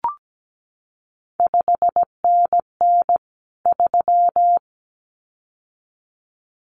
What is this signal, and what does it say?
Morse code, heard as a keyed tone, about 17 wpm, 700 Hz. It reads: 5NN 3
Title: CQ WW DX Contest Exchange Encore - 17wpm